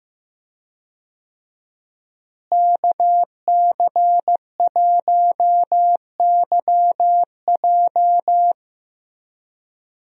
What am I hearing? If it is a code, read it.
Morse code, heard as a keyed tone, about 15 wpm, 700 Hz. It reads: KC1YJ
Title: USA Call Signs - 5 Characters - Repeated 3x - 15wpm